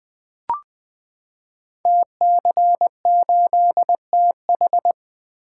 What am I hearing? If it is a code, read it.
Morse code, heard as a keyed tone, about 20 wpm, 700 Hz. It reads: TC8TH
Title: Code Groups - 5 Letter:Numbers - 20wpm